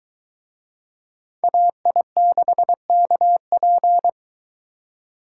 Morse code, heard as a keyed tone, about 21 wpm, 700 Hz. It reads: AI6KP